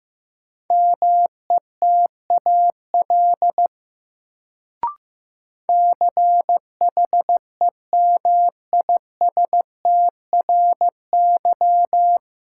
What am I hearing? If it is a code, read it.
Morse code, heard as a keyed tone, about 15 wpm, 700 Hz. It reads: METAL CHEMISTRY
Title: Sentences from Top 1000 Words - Review Part 6 - New Word Drill - 15wpm